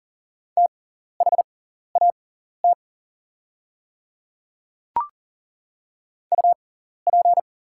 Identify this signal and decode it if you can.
Morse code, heard as a keyed tone, about 40 wpm, spaced as for 14 wpm, 700 Hz. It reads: THAT UP
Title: 6x Character Spacing - Top 100 Words - 40wpm